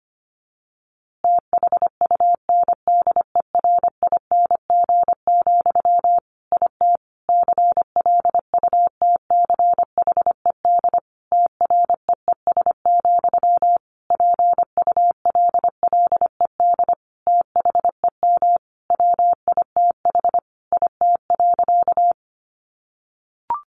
Morse code, heard as a keyed tone, about 25 wpm, 700 Hz. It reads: THUNDERING, IT CLUTCHED TREES, PULLED THEM WITH IT.
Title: A Sound Of Thunder 25wpm